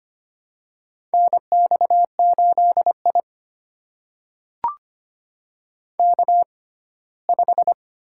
Morse code, heard as a keyed tone, about 25 wpm, spaced as for 17 wpm, 700 Hz. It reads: NX8I K5
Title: USA Call Signs - 4 Characters - 6x Character Spacing - 25wpm